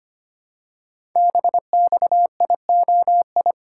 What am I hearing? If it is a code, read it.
Morse code, heard as a keyed tone, about 25 wpm, 700 Hz. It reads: BXIOI
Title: Code Groups - 5 Letters - 25wpm